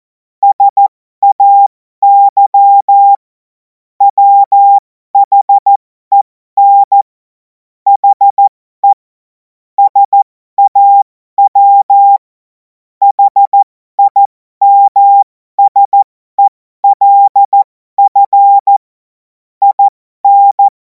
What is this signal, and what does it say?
Morse code, heard as a keyed tone, about 14 wpm, spaced as for 12 wpm, 800 Hz. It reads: SAY WHEN HE SAW HIMSELF IN
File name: Morsekurs / Witze